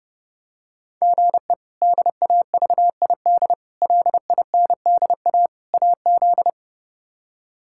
Morse code, heard as a keyed tone, about 32 wpm, 700 Hz. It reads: GE DAVID LINDA AZ
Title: K1USN SST QSO Practice - 30wpm